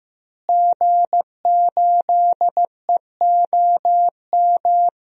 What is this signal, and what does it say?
Morse code, heard as a keyed tone, about 15 wpm, 700 Hz. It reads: G8EOM
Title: DXCC-100 Call Signs - 5 Characters - Repeated 2x - 15wpm